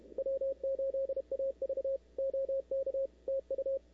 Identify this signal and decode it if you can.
Morse code, heard as a keyed tone, about 31 wpm, 530 Hz, with radio noise. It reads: W8AV OK TU
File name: CW / 2005-k5tr arrl cw dx -03:00